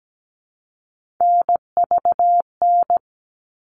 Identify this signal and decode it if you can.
Morse code, heard as a keyed tone, about 17 wpm, 700 Hz. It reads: NVN
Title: All Cut Numbers - 3-Digit 17wpm